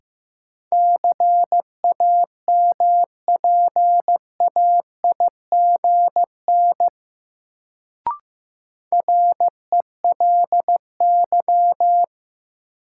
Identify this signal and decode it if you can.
Morse code, heard as a keyed tone, about 15 wpm, 700 Hz. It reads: CAMPAIGN RELY
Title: Sentences from Top 1200 Words - New Word Drill - 15wpm